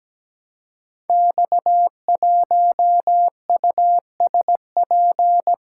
Morse code, heard as a keyed tone, about 17 wpm, 700 Hz. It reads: X1USP
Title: Code Groups - 5 Letter:Numbers - 17wpm